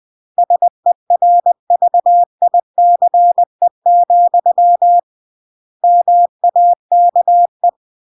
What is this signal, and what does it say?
Morse code, heard as a keyed tone, about 20 wpm, 700 Hz. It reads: SERVICE, MAKE
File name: CW / jodorowsky / 82 commandments 20-20